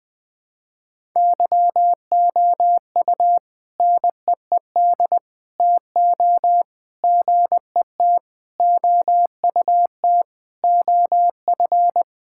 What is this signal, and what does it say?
Morse code, heard as a keyed tone, about 20 wpm, 700 Hz. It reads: YOU NEED TO GET OUT OF
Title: Sentences with 4-Word Phrases - 20wpm